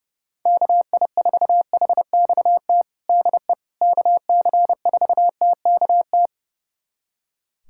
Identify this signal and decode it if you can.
Morse code, heard as a keyed tone, about 30 wpm, 700 Hz. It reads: KI4HXT DE KC4TKT
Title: USA Callsign DE Callsign - 6 Characters - 30wpm